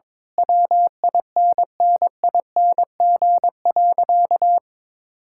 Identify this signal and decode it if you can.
Morse code, heard as a keyed tone, about 21 wpm, 700 Hz. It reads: WINNING.
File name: practice / 22wpm / Sentences with 8-Word Phrases - 22wpm